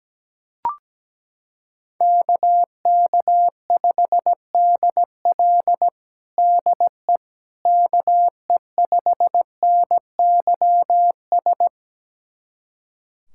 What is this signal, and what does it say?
Morse code, heard as a keyed tone, about 17 wpm, 700 Hz. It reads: KK5DL DE KE5NYS
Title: USA Callsign DE Callsign 17wpm